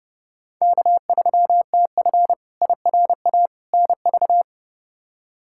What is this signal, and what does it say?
Morse code, heard as a keyed tone, about 31 wpm, 700 Hz. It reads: K3TF IRA NV
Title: K1USN SST QSO Practice - 30wpm